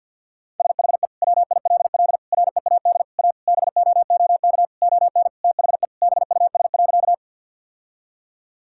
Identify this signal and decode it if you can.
Morse code, heard as a keyed tone, about 51 wpm, 700 Hz. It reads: SHE WILL READ A BOOK ON THE BUS.
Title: Sentences from Top 1000 Words - Review Part 6 - 50wpm